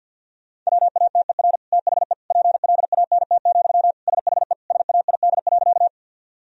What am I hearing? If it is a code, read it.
Morse code, heard as a keyed tone, about 53 wpm, 700 Hz. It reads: WATER THE PLANT, SHE SAID.
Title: Sentences from Top 1000 Words - Review Part 1 - 50wpm